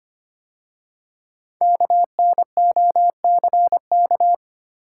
Morse code, heard as a keyed tone, about 25 wpm, 700 Hz. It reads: KNOCK
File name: 5 Letter Words - First 3 Letters Given - 25wpm